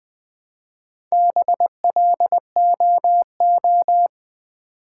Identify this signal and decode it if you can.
Morse code, heard as a keyed tone, about 20 wpm, 700 Hz. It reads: BLOO
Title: Build A Word - 1 Letter At A Time - 3 to 5-Letter Words - 20wpm